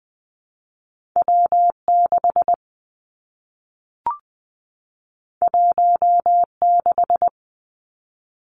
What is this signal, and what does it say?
Morse code, heard as a keyed tone, about 20 wpm, 700 Hz. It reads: W 6 1 6
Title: Two Challenging Letter-Number Combinations 20wpm